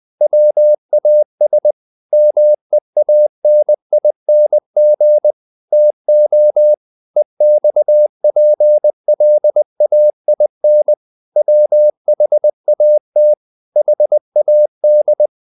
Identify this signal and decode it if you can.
Morse code, heard as a keyed tone, about 20 wpm, 600 Hz. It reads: WAS MEANING TO EXPLAIN WHAT HAD
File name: the old man and the sea part 2 20wpm